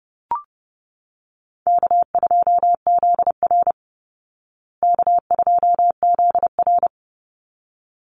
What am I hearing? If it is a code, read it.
Morse code, heard as a keyed tone, about 28 wpm, 700 Hz. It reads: K2ZR K2ZR
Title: Easy Call Signs Repeated 2x - 30wpm